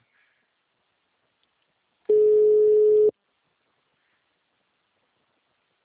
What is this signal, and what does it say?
Ringback tone (Europe, 425 Hz).